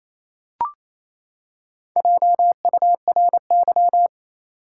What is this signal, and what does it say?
Morse code, heard as a keyed tone, about 28 wpm, 700 Hz. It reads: JURY
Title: Four Letter Words - 28wpm